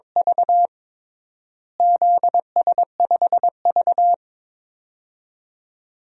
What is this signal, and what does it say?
Morse code, heard as a keyed tone, about 22 wpm, 700 Hz. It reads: V ZS5V
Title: DXCC-100 Call Signs - 4 Characters - Repeated 2x - 22wpm